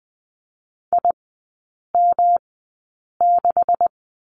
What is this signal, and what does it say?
Morse code, heard as a keyed tone, about 20 wpm, spaced as for 8 wpm, 700 Hz. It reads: IM6
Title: 166 Lesson 27e - Number 7 - Three Characters 20wpm